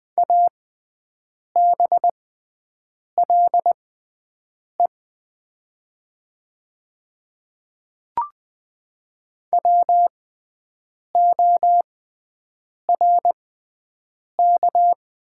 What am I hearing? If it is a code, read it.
Morse code, heard as a keyed tone, about 20 wpm, spaced as for 7 wpm, 700 Hz. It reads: ABLE WORK
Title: 6x Character Spacing - Top 500 Words - 20wpm